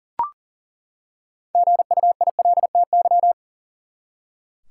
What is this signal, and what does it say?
Morse code, heard as a keyed tone, about 40 wpm, 700 Hz. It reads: GUILTY
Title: Six Letter Words - 40wpm